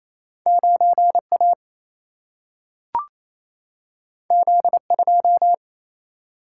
Morse code, heard as a keyed tone, about 28 wpm, 700 Hz. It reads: 9 A Z 2
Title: Single-Letter Single-Number 28wpm